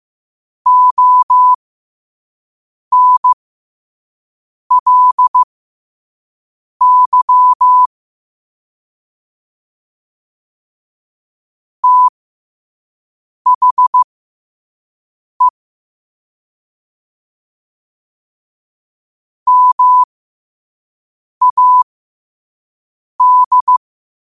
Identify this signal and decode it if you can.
Morse code, heard as a keyed tone, about 15 wpm, spaced as for 5 wpm, 1000 Hz. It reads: ONLY THE MAD
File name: qotdfiles / 5wpm